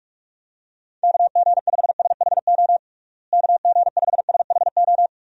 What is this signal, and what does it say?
Morse code, heard as a keyed tone, about 42 wpm, 700 Hz. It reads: KG5SHO KG5SHO
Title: USA Call Signs - 6 Characters - Repeated 2x - 45wpm